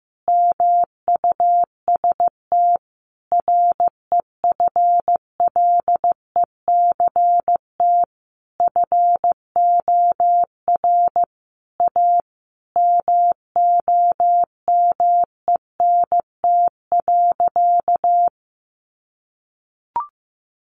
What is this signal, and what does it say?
Morse code, heard as a keyed tone, about 15 wpm, 700 Hz. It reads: MUST REFLECT FOR A MOMENT.
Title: Sentences from Top 1500 Words 15wpm